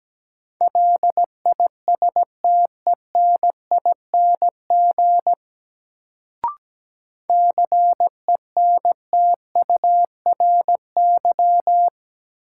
Morse code, heard as a keyed tone, about 17 wpm, 700 Hz. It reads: LISTENING CENTURY
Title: Sets of 5 Words - New Word Drill - 17wpm